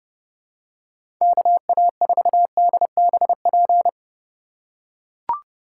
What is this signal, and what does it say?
Morse code, heard as a keyed tone, about 30 wpm, 700 Hz. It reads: KA4DBP
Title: USA Call Signs - 6 Characters - 2x Character Spacing - 30wpm